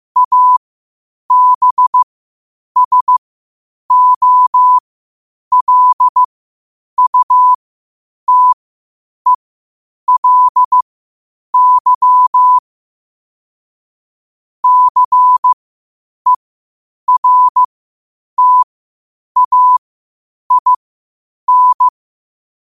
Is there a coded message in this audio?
Yes: Morse code from 0.2 to 21.9 s, heard as a keyed tone, about 15 wpm, spaced as for 8 wpm, 1000 Hz. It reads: ABSOLUTELY CERTAIN